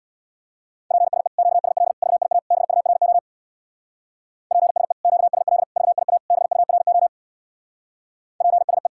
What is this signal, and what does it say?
Morse code, heard as a keyed tone, about 64 wpm, 700 Hz. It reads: PSE QSL VIA BURO PSE QSL VIA BURO PSE